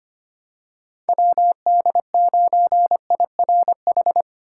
Morse code, heard as a keyed tone, about 25 wpm, 700 Hz. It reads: WD9IRH
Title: USA Call Signs - 2x Character Spacing - 25wpm